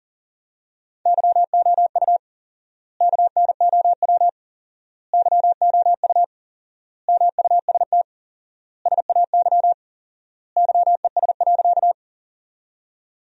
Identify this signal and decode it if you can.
Morse code, heard as a keyed tone, about 40 wpm, 700 Hz. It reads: YOU KNOW YOU MUST SAY YES.
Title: 4x Word-Spacing - Sentences from Top 500 Words - 40wpm